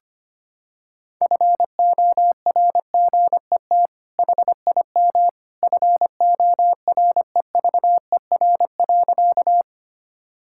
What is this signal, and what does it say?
Morse code, heard as a keyed tone, about 25 wpm, 700 Hz. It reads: FORGET HIM FOREVER.
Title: Sentences from Top 1600 Words 25wpm